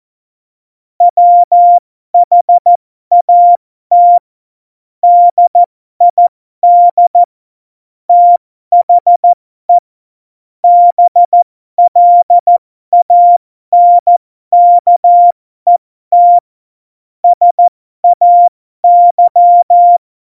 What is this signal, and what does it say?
Morse code, heard as a keyed tone, about 14 wpm, spaced as for 12 wpm, 700 Hz. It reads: WHAT DID THE BLANKET SAY
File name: Morsekurs / Witze